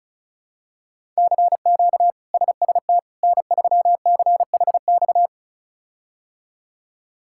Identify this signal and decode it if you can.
Morse code, heard as a keyed tone, about 36 wpm, 700 Hz. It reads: CQ SST N3CHX
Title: K1USN SST QSO Practice - 35wpm